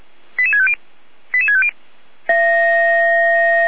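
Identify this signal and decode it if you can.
Five-tone selective call (ZVEI), heard as quick runs of five tones, tones of about 70 ms. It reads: 88650 88650
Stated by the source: ZVEI five-tone selective call received on SDR.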